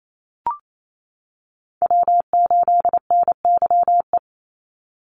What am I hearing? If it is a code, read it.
Morse code, heard as a keyed tone, about 28 wpm, 700 Hz. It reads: W8NYE
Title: USA Call Signs 28wpm